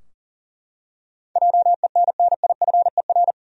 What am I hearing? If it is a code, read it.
Morse code, heard as a keyed tone, about 40 wpm, 700 Hz. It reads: JENNIFER
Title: USA Common Names - 40wpm